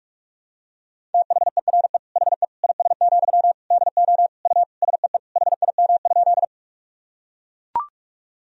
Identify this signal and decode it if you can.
Morse code, heard as a keyed tone, about 43 wpm, 700 Hz. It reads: THERE HE IS, DO U SEE HIM?
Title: Sentences from Top 100 Words and Abbreviations 45wpm